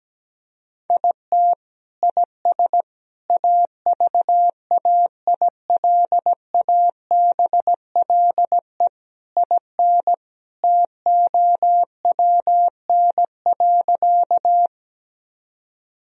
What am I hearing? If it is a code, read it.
Morse code, heard as a keyed tone, about 17 wpm, 700 Hz. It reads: IT IS AVAILABLE IN TOWN.
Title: Sentences from Top 1000 Words - Review Part 3 - 17wpm